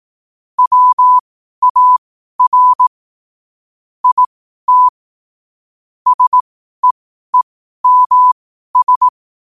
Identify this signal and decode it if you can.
Morse code, heard as a keyed tone, about 18 wpm, spaced as for 12 wpm, 1000 Hz. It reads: WAR IT SEEMS
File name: qotdfiles / 12wpm